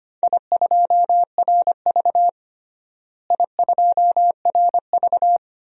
Morse code, heard as a keyed tone, about 25 wpm, 700 Hz. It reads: I2RV I2RV
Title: DXCC-100 Call Signs - 4 Characters - Repeated 2x - 25wpm